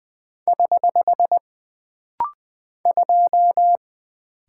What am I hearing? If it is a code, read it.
Morse code, heard as a keyed tone, about 20 wpm, 700 Hz. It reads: <HH> 2
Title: Single Character (Letter:Number:Punctuation:Prosign) - Rapid-Fire - 20wpm